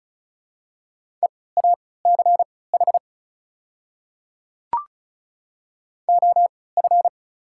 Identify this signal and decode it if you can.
Morse code, heard as a keyed tone, about 36 wpm, spaced as for 20 wpm, 690 Hz. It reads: EACH OF
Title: 3x Character Spacing - Top 100 Words - 35wpm